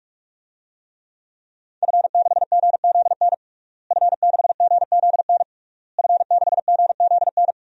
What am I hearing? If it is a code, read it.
Morse code, heard as a keyed tone, about 41 wpm, 700 Hz. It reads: F6GZN F6GZN F6GZN
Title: DXCC-100 Call Signs - 5 Characters - Repeated 3x - 45wpm